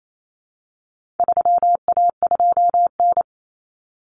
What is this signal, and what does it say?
Morse code, heard as a keyed tone, about 28 wpm, 700 Hz. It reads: 3A2N
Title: Common Cut Numbers - 4-Digit 28wpm